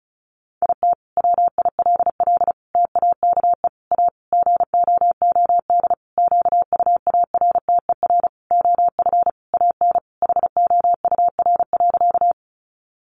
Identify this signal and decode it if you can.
Morse code, heard as a keyed tone, about 34 wpm, 700 Hz. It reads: IT WILL TAKE A GOOD QUARTER OF AN HOUR.
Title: Sentences from Top 1300 Words 35wpm